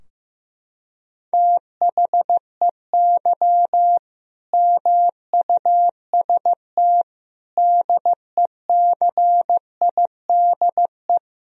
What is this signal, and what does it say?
Morse code, heard as a keyed tone, about 15 wpm, 700 Hz. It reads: THEY MUST DECIDE
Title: Sentences from Top 1000 Words - Review Part 3 - 15wpm